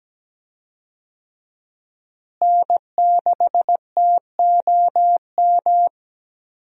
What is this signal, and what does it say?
Morse code, heard as a keyed tone, about 17 wpm, 700 Hz. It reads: N6TOM
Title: USA Call Signs - 5 Characters - Repeated 2x - 17wpm